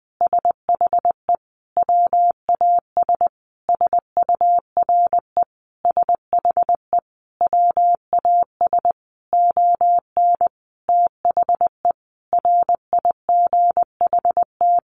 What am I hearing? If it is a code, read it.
Morse code, heard as a keyed tone, about 19 wpm, 700 Hz. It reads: SHE WAS SURE SHE WAS ON THE RIGHT
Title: Sentences from Top 1100 Words 20wpm